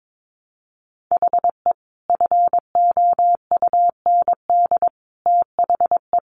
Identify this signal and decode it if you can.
Morse code, heard as a keyed tone, about 22 wpm, 700 Hz. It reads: HE FOUND THE
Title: Sentences from Top 1200 Words 22wpm